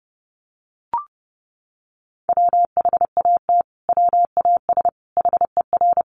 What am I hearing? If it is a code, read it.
Morse code, heard as a keyed tone, about 30 wpm, 700 Hz. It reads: WHAT WAS HER
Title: Sentences from Top 1400 Words 30wpm